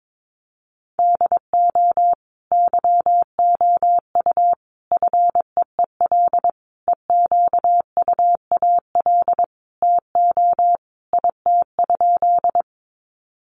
Morse code, heard as a keyed tone, about 22 wpm, 700 Hz. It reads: DO YOU FEEL EQUAL TO IT?
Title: Sentences from Top 1400 Words 22wpm